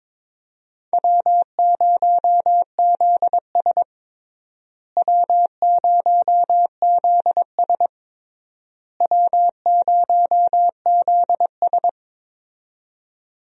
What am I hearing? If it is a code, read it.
Morse code, heard as a keyed tone, about 22 wpm, 700 Hz. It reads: W0ZS W0ZS W0ZS
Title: USA Call Signs - 4 Characters - Repeated 3x - 22wpm